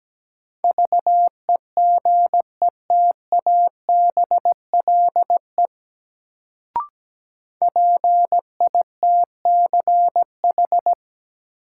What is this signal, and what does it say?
Morse code, heard as a keyed tone, about 17 wpm, 700 Hz. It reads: VEGETABLE PITCH
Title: Sentences from Top 1300 Words - New Word Drill - 17wpm